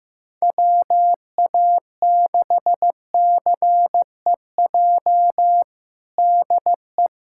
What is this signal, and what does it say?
Morse code, heard as a keyed tone, about 15 wpm, 700 Hz. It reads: WA6CEJ DE
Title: USA Callsign DE Callsign - 6 Characters - 15wpm